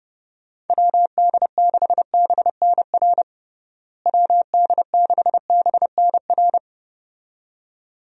Morse code, heard as a keyed tone, about 31 wpm, 700 Hz. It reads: WD6BNR WD6BNR
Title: USA Call Signs - 6 Characters - Repeated 2x - 30wpm